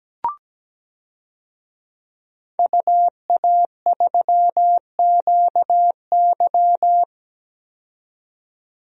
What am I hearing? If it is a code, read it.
Morse code, heard as a keyed tone, about 17 wpm, 700 Hz. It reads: UA3QY
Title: DXCC-100 Call Signs - 5 Characters - 17wpm